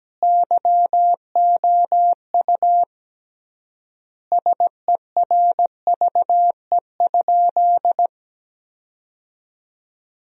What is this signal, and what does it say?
Morse code, heard as a keyed tone, about 17 wpm, 700 Hz. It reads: YOU SERVE?
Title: 3x Word-Spacing - Sentences from Top 400 Words - 17wpm